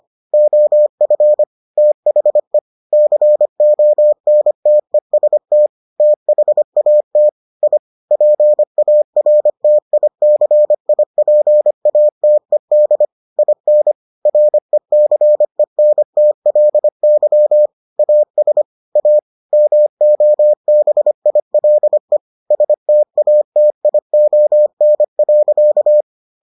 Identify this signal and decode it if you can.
Morse code, heard as a keyed tone, about 25 wpm, 600 Hz. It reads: OF THE CONTEST THAT I PARTICIPATED IN RECENTLY AS A MOBILE STATION.